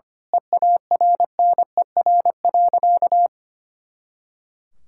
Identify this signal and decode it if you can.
Morse code, heard as a keyed tone, about 25 wpm, 700 Hz. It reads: EARNER.
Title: Pangram Challenge v2 - 25wpm